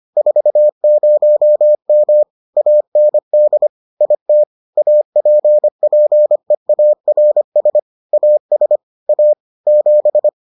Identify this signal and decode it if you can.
Morse code, heard as a keyed tone, about 25 wpm, 600 Hz. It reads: 40M AND IT APPEARS AS A 7